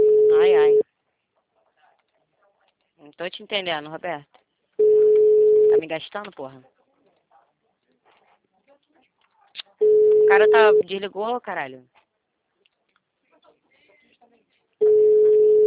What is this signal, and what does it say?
Ringback tone (Europe, 425 Hz), with speech.